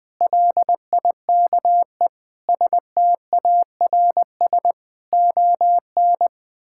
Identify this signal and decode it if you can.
Morse code, heard as a keyed tone, about 20 wpm, 700 Hz. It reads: LIKE STARS ON
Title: Difficult Sentences with Hints - 20wpm